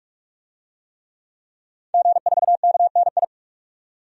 Morse code, heard as a keyed tone, about 46 wpm, 700 Hz. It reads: G4KNI
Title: DXCC-100 Call Signs - 5 Characters - 45wpm